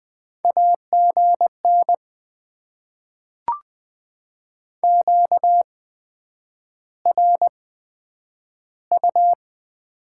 Morse code, heard as a keyed tone, about 20 wpm, spaced as for 9 wpm, 700 Hz. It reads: AGN QRU
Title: Top 90 QSO Elements - 8x Character Spacing - 20wpm